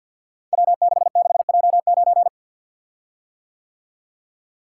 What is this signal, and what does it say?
Morse code, heard as a keyed tone, about 53 wpm, 700 Hz. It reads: W66J9
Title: Code Groups - 5 Letter:Numbers - 50wpm